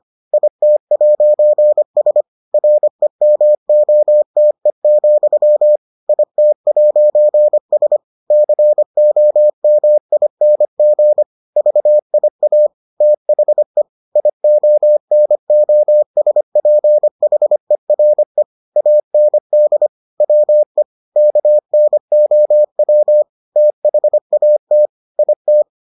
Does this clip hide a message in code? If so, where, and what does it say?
Yes: Morse code from 0.3 to 25.6 s, heard as a keyed tone, about 25 wpm, 600 Hz. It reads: IT'S REMOTE, IT'S COMING VIA THE IONOSPHERE AND WE KNOW THAT IT